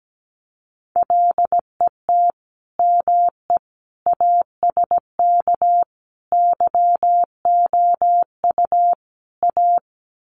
Morse code, heard as a keyed tone, about 17 wpm, 700 Hz. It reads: LET ME ASK YOU A
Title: Sets of 5 Words Encore 17wpm